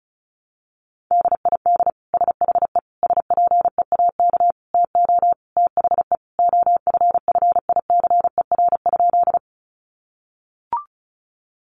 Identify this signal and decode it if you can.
Morse code, heard as a keyed tone, about 34 wpm, 700 Hz. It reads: DID SHE SPEAK TO THE OFFICER?
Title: Sentences from Top 1200 Words 35wpm